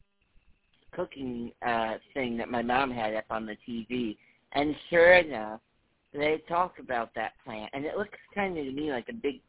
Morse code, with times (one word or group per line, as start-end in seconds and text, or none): none